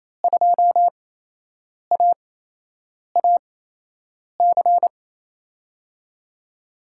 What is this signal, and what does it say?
Morse code, heard as a keyed tone, about 28 wpm, spaced as for 8 wpm, 700 Hz. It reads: 2AAC
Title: USA Call Signs - 5 Characters - 8x Character Spacing - 28wpm